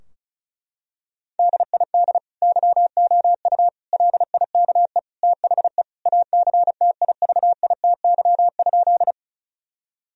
Morse code, heard as a keyed tone, about 36 wpm, 700 Hz. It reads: DID YOU LIKE THE ACTIVITY?
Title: Sentences from Top 1000 Words - Review Part 4 - 35wpm